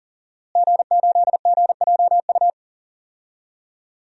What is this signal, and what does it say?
Morse code, heard as a keyed tone, about 40 wpm, 700 Hz. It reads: G8GJU
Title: DXCC-100 Call Signs - 5 Characters - Repeated 2x - 40wpm